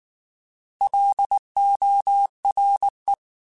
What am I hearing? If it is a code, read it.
Morse code, heard as a keyed tone, about 19 wpm, 800 Hz. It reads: LORE
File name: Alphabetical / Alphabetical mp3s / Ls @18